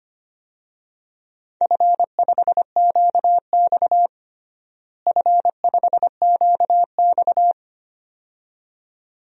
Morse code, heard as a keyed tone, about 25 wpm, 700 Hz. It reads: F5QX F5QX
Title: DXCC-100 Call Signs - 4 Characters - Repeated 2x - 25wpm